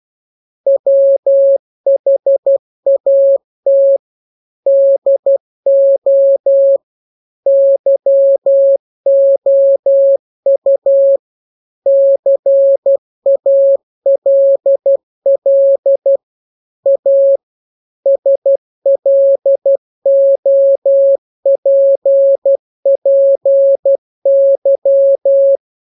Morse code, heard as a keyed tone, about 12 wpm, 550 Hz. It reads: WHAT DO YOU CALL A SLOPPY